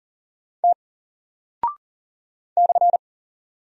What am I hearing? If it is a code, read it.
Morse code, heard as a keyed tone, about 40 wpm, 700 Hz. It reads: T /
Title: Single Character (Letter:Number:Punctuation:Prosign) Alt 40wpm